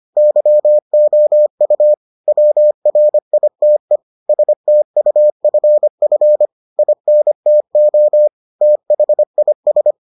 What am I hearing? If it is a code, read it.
Morse code, heard as a keyed tone, about 25 wpm, 600 Hz. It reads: YOU WRITE STUFF INTO THIS